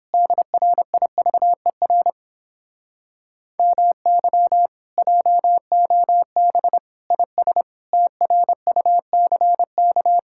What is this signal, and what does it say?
Morse code, heard as a keyed tone, about 29 wpm, 700 Hz. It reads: DRIVER MY JOB IS TRUCK